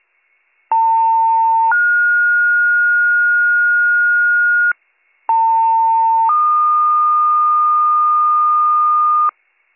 Two-tone page, heard as a one-second tone, then a three-second tone, about 900 then 1430 Hz, about 900 then 1220 Hz, with noise. It is heard from 0.7 to 9.3 s.